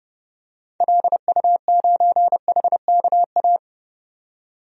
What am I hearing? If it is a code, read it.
Morse code, heard as a keyed tone, about 30 wpm, 700 Hz. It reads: LU9HKA
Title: DXCC-100 Call Signs - 6 Characters - 30wpm